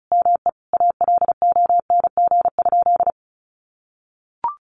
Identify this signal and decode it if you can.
Morse code, heard as a keyed tone, about 34 wpm, 700 Hz. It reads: ME ALONG?
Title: Sentences from Top 1300 Words 35wpm